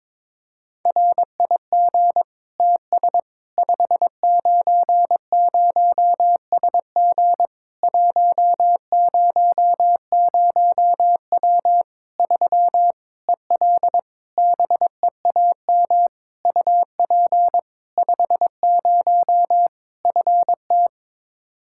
Morse code, heard as a keyed tone, about 22 wpm, 700 Hz. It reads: RIG TS 590SG 100W 3 EL BEAM UP 50 FT
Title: K1USN SST QSO Practice 22wpm